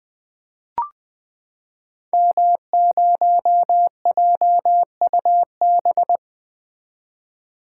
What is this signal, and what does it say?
Morse code, heard as a keyed tone, about 20 wpm, 700 Hz. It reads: M0JUB
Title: Code Groups - 5 Letter:Numbers - 20wpm